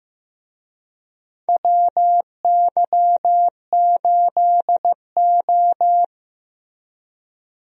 Morse code, heard as a keyed tone, about 15 wpm, 700 Hz. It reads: WY8O
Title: USA Call Signs - 4 Characters - Repeated 3x - 15wpm